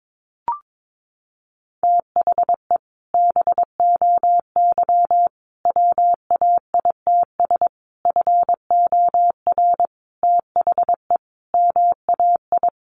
Morse code, heard as a keyed tone, about 22 wpm, 700 Hz. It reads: THE BOY WAITS FOR THE MAI
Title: Additional Easy Phrases 22wpm